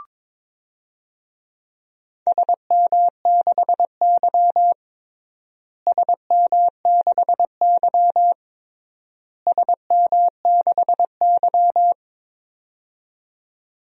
Morse code, heard as a keyed tone, about 21 wpm, 700 Hz. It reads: SM6Y SM6Y SM6Y
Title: DXCC-100 Call Signs - 4 Characters - Repeated 3x - 22wpm